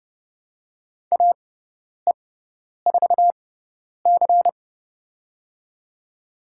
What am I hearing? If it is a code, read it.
Morse code, heard as a keyed tone, about 30 wpm, spaced as for 10 wpm, 690 Hz. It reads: AE4C
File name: Lesson 18g - Letter m - Call Signs 30wpm